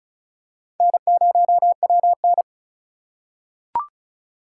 Morse code, heard as a keyed tone, about 34 wpm, 700 Hz. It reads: N0WN
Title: USA Call Signs - 6x Character Spacing - 35wpm